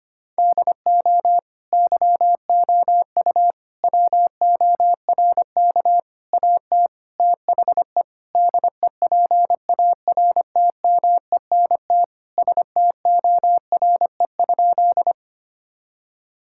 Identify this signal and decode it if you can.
Morse code, heard as a keyed tone, about 25 wpm, 700 Hz. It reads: DO YOU WORK AT THE DEPARTMENT STORE?
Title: Sentences from Top 1000 Words - Review Part 5 - 25wpm